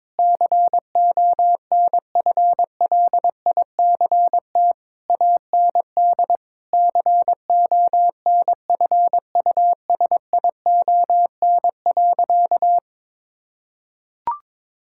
Morse code, heard as a keyed tone, about 22 wpm, 700 Hz. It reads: CONFLICT AND CONFUSION.
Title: Sentences from Top 1600 Words 22wpm